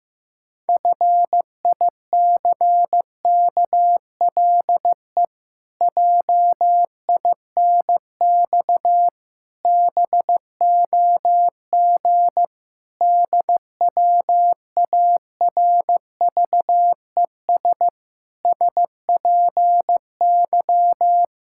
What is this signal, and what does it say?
Morse code, heard as a keyed tone, about 15 wpm, 700 Hz. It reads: FICKLE JINX BOG DWARVES SPY
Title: Pangram Challenge v2 - 15wpm